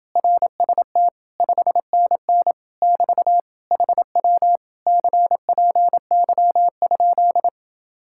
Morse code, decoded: RST 5NN <BT> HW CPY?

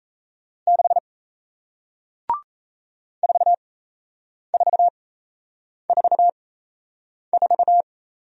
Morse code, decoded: 6 4444

